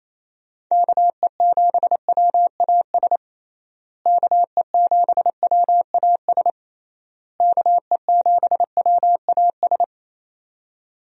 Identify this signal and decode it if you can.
Morse code, heard as a keyed tone, about 28 wpm, 700 Hz. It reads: KE7WAS KE7WAS KE7WAS
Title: USA Call Signs - 6 Characters - Repeated 3x - 28wpm